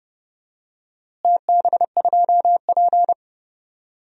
Morse code, heard as a keyed tone, about 30 wpm, 700 Hz. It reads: TB2P
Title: DXCC-100 Call Signs - 4 Characters - Repeated 2x - 30wpm